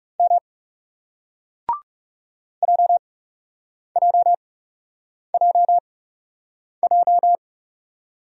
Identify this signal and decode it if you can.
Morse code, heard as a keyed tone, about 40 wpm, spaced as for 8 wpm, 700 Hz. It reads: M JJJJ